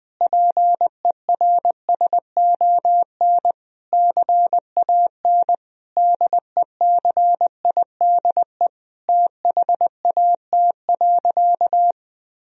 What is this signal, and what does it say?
Morse code, heard as a keyed tone, about 20 wpm, 700 Hz. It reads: PERSON CAN DECIDE THAT.
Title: Sentences from Top 1000 Words - Review Part 3 - 20wpm